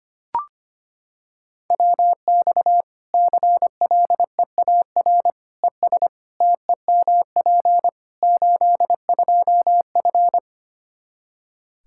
Morse code, heard as a keyed tone, about 25 wpm, 700 Hz. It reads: WX CLEAR ES TEMP 82F
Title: Practice QSO - Exchange 2 - Info - 25wpm